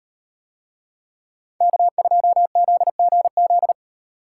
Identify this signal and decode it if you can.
Morse code, heard as a keyed tone, about 38 wpm, 700 Hz. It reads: K2ZGZ